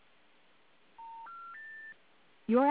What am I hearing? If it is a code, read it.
Special information tone, with noise.